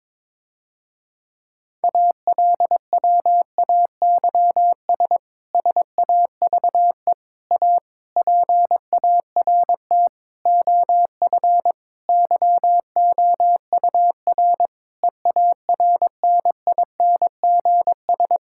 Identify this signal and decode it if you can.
Morse code, heard as a keyed tone, about 22 wpm, 700 Hz. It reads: ALWAYS SAVE A PART OF YOUR EARNINGS
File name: Difficult Sentences with Hints - 22wpm